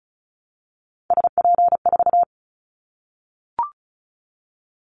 Morse code, heard as a keyed tone, about 36 wpm, 700 Hz. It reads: SP4
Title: Callsign Prefixes for Top 25 DXCC - 35wpm